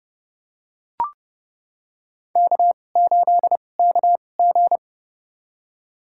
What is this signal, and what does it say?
Morse code, heard as a keyed tone, about 31 wpm, spaced as for 22 wpm, 700 Hz. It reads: K8KG
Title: USA Call Signs - 4 Characters - 2x Character Spacing - 30wpm